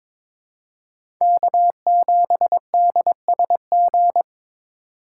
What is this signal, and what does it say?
Morse code, heard as a keyed tone, about 22 wpm, 700 Hz. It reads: K7DSG
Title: USA Call Signs - 5 Characters - 2x Character Spacing - 22wpm